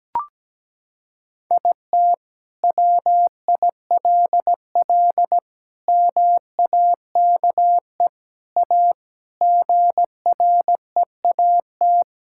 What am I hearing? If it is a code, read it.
Morse code, heard as a keyed tone, about 17 wpm, 700 Hz. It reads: IT WILL MAKE A GREAT
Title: Sentences from Top 1000 Words - Review Part 5 - 17wpm